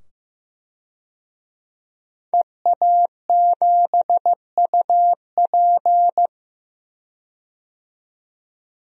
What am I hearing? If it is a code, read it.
Morse code, heard as a keyed tone, about 15 wpm, 700 Hz. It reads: EA7UP
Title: DXCC-100 Call Signs - 5 Characters - 15wpm